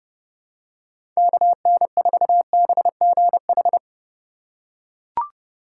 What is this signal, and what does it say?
Morse code, heard as a keyed tone, about 30 wpm, 700 Hz. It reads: KN4BGH
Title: USA Call Signs - 6 Characters - 4x Character Spacing - 30wpm